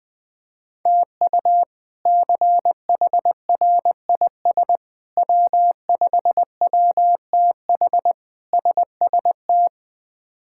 Morse code, heard as a keyed tone, about 21 wpm, 700 Hz. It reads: TU CHRIS W5WTH SST
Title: K1USN SST QSO Practice - 20wpm